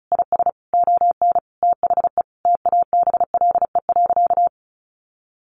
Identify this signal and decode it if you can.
Morse code, heard as a keyed tone, about 34 wpm, 700 Hz. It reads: IS ON THE TABLE.
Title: Sentences from Top 1200 Words 35wpm